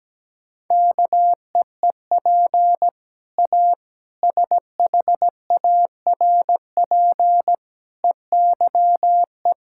Morse code, heard as a keyed tone, about 17 wpm, 700 Hz. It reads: KEEP A SHARP EYE
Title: Sentences from Top 1600 Words 17wpm